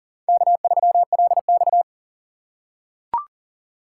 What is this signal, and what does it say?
Morse code, heard as a keyed tone, about 39 wpm, 690 Hz. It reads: K3LX
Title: USA Call Signs - 4 Characters - 2x Character Spacing - 40wpm